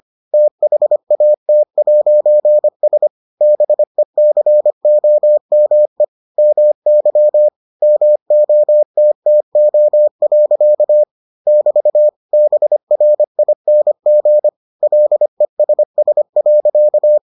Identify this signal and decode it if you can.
Morse code, heard as a keyed tone, about 25 wpm, 600 Hz. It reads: THAT'S BECOME MY MOTTO. <BT> BRING LESS.